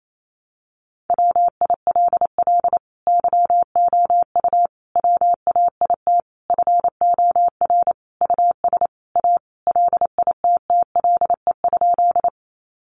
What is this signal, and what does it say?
Morse code, heard as a keyed tone, about 28 wpm, 700 Hz. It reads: WILL YOU WAIT FOR US A LITTLE?
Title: Sentences from Top 900 Words 28wpm